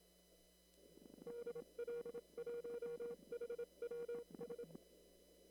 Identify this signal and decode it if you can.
Morse code, heard as a keyed tone, about 26 wpm, 480 Hz. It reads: DL1HWS
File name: CW / kt5j wpx cw-op-k5pi